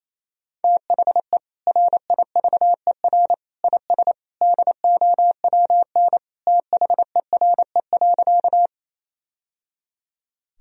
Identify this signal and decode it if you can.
Morse code, heard as a keyed tone, about 28 wpm, 700 Hz. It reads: THE RIVER IS DOWN THERE.
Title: Sentences from Top 1000 Words - Review Part 1 - 28wpm